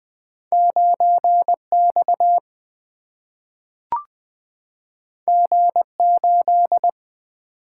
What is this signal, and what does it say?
Morse code, heard as a keyed tone, about 20 wpm, 700 Hz. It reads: 9X G8
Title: Single-Letter Single-Number 20wpm